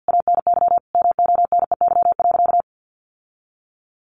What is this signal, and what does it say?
Morse code, heard as a keyed tone, about 50 wpm, 700 Hz. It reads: ANY MONEY.
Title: Sentences from Top 1500 Words 50wpm